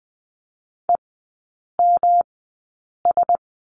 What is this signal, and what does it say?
Morse code, heard as a keyed tone, about 19 wpm, spaced as for 8 wpm, 700 Hz. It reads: EMS
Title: 121 Lesson 20e - Number 3 - Three Characters 20wpm